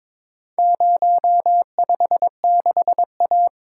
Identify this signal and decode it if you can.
Morse code, heard as a keyed tone, about 22 wpm, 700 Hz. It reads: 056A
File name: practice / 22wpm / Four Letter-Number Combinations - 22wpm